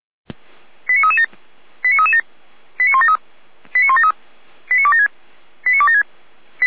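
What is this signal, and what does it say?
Five-tone selective call (ZVEI), heard as quick runs of five tones, tones of about 70 ms. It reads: 89337 89337 89173 89173 89276 89276
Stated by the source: ZVEI five-tone selective call received on an SDR.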